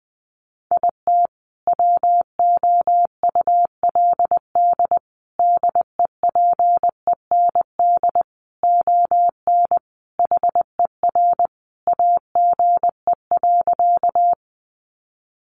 Morse code, heard as a keyed tone, about 19 wpm, 700 Hz. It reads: IT WOULD DEPEND ON HER AGE.
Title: Sentences from Top 1300 Words 20wpm